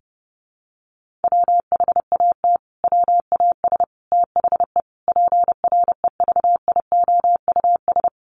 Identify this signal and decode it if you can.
Morse code, heard as a keyed tone, about 28 wpm, 700 Hz. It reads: WHAT WAS THE PREVIOUS
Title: Sentences from Top 1500 Words 30wpm